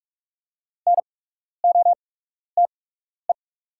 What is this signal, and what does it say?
Morse code, heard as a keyed tone, about 44 wpm, spaced as for 12 wpm, 700 Hz. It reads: NOTE
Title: 8x Character Spacing - Top 500 Words - 45wpm